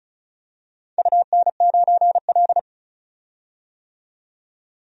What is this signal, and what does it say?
Morse code, heard as a keyed tone, about 36 wpm, 700 Hz. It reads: UN9L
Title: DXCC-100 Call Signs - 4 Characters - 35wpm